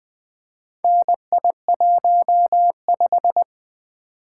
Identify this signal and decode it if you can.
Morse code, heard as a keyed tone, about 21 wpm, 700 Hz. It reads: NI15
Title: Four Letter-Number Combinations - 20wpm